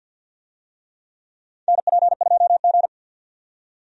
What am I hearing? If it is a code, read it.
Morse code, heard as a keyed tone, about 50 wpm, 700 Hz. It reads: NP2G
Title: USA Call Signs - 4 Characters - 50wpm